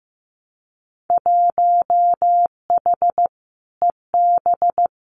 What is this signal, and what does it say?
Morse code, heard as a keyed tone, about 15 wpm, 700 Hz. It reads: 1H EB
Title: Winter Field Day Exchanges 15wpm